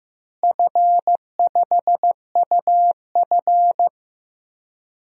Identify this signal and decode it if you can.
Morse code, heard as a keyed tone, about 15 wpm, 700 Hz. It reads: F5UF
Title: DXCC-100 Call Signs - 4 Characters - Repeated 3x - 15wpm